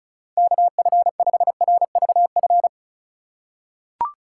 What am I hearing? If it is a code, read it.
Morse code, heard as a keyed tone, about 36 wpm, 700 Hz. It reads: KF5RVF
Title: USA Call Signs - 3x Character Spacing - 35wpm